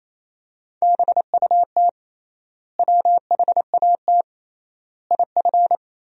Morse code, heard as a keyed tone, about 27 wpm, 700 Hz. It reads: BUT WHAT IF
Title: 3x Word-Spacing - Sentences from Top 100 Words - 28wpm